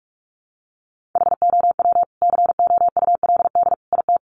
Morse code, heard as a keyed tone, about 44 wpm, 700 Hz. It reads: HOW COULD IT
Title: Sentences from Top 1400 Words 45wpm